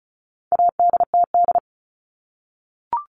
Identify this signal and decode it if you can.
Morse code, heard as a keyed tone, about 34 wpm, 700 Hz. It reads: ADTD
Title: All Cut Numbers - 4-Digit 35wpm